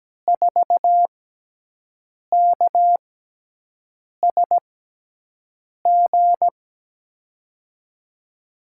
Morse code, heard as a keyed tone, about 17 wpm, spaced as for 6 wpm, 700 Hz. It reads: 4KSG
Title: USA Call Signs - 6x Character Spacing - 17wpm